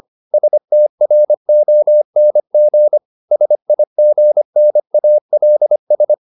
Morse code, heard as a keyed tone, about 25 wpm, 600 Hz. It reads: STRONG SIGNALS